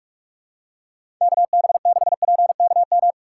Morse code, heard as a keyed tone, about 46 wpm, 700 Hz. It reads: KB6PKM
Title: USA Call Signs - 6 Characters - 4x Character Spacing - 45wpm